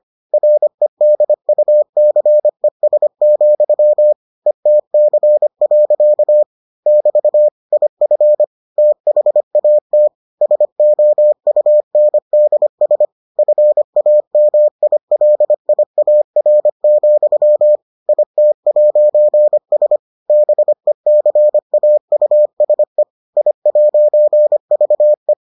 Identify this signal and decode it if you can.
Morse code, heard as a keyed tone, about 25 wpm, 600 Hz. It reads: REDUCES, ETC. <BT> IF THAT SOUNDS FAMILIAR, IT'S BECAUSE I'VE